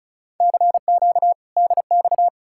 Morse code, heard as a keyed tone, about 46 wpm, 700 Hz. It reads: CQ DX